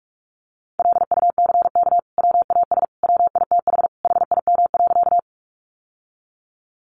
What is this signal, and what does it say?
Morse code, heard as a keyed tone, about 46 wpm, 700 Hz. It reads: LUCK WAS WITH HIM.
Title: Sentences from Top 1500 Words 45wpm